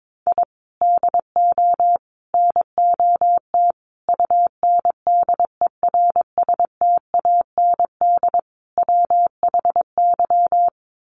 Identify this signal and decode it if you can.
Morse code, heard as a keyed tone, about 22 wpm, 700 Hz. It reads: I DO NOT UNDERSTAND WHY
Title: Sets of 5 Words Encore 22wpm